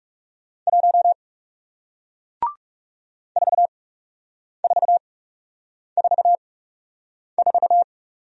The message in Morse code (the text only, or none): 1 4444